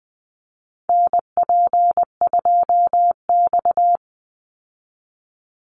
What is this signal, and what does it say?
Morse code, heard as a keyed tone, about 20 wpm, 700 Hz. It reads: NP2X
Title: Call Signs Repeated 2x - 20wpm